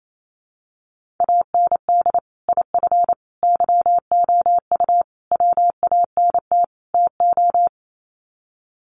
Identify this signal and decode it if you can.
Morse code, heard as a keyed tone, about 28 wpm, 700 Hz. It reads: AND IF YOU WANT TO
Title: Sets of 5 Words Encore 28wpm